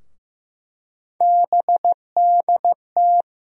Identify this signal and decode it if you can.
Morse code, heard as a keyed tone, about 15 wpm, 700 Hz. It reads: BDT
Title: USA Call Sign Postfixes - 3 Letters 15wpm